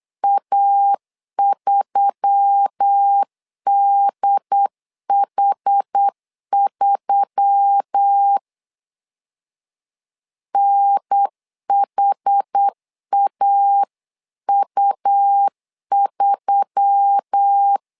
Morse code, heard as a keyed tone, about 8 wpm, 800 Hz. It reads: A3DH3 NHAU3
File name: LEZIONI / 4 Q1S3 8wpm 3037